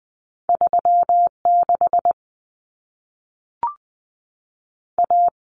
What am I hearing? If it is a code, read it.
Morse code, heard as a keyed tone, about 20 wpm, 700 Hz. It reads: 36 A
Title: Common Cut Numbers - 2-Digit 20wpm